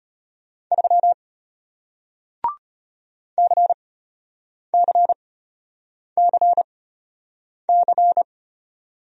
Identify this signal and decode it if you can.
Morse code, heard as a keyed tone, about 34 wpm, spaced as for 8 wpm, 700 Hz. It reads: 3 CCCC